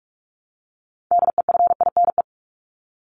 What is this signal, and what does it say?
Morse code, heard as a keyed tone, about 44 wpm, 700 Hz. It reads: DEFINE
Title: ED Suffix - Top 100 Words with Base Words 45wpm